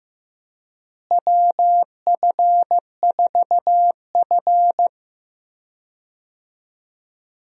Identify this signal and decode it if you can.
Morse code, heard as a keyed tone, about 15 wpm, 700 Hz. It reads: WF4F
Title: USA Call Signs - 4 Characters - Repeated 3x - 15wpm